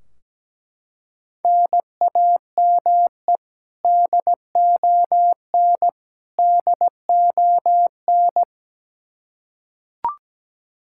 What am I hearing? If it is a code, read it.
Morse code, heard as a keyed tone, about 17 wpm, 700 Hz. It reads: NAME DON DON
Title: Practice QSO - Exchange 1 - Beginning - 17wpm